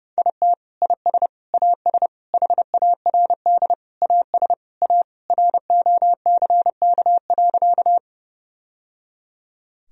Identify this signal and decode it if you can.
Morse code, heard as a keyed tone, about 32 wpm, 700 Hz. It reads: IT IS AS HARD AS A ROCK.
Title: Sentences from Top 1000 Words - Review Part 2 - 30wpm